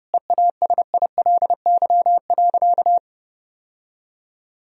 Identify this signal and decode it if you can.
Morse code, heard as a keyed tone, about 30 wpm, 700 Hz. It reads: EASILY.
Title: Sentences from Top 1000 Words - Review Part 3 - 30wpm